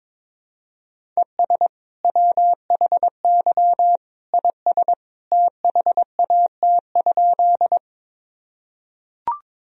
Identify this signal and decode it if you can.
Morse code, heard as a keyed tone, about 21 wpm, 700 Hz. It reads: ES WHY IS THAT?
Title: Sentences from Top 200 Words and Abbreviations 22wpm